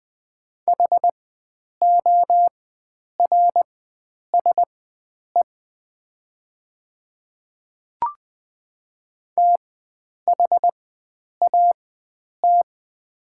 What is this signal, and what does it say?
Morse code, heard as a keyed tone, about 20 wpm, spaced as for 9 wpm, 700 Hz. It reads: HORSE THAT
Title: 4x Character Spacing - Top 300 Words - 20wpm